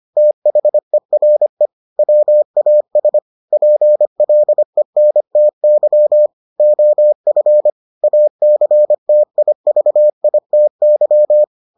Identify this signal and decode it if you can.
Morse code, heard as a keyed tone, about 25 wpm, 600 Hz. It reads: THERE WAS PLENTY OF ACTIVITY